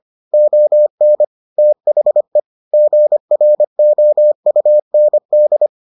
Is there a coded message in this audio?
Yes: Morse code from 0.3 to 5.7 s, heard as a keyed tone, about 25 wpm, 600 Hz. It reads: ON THE GROUND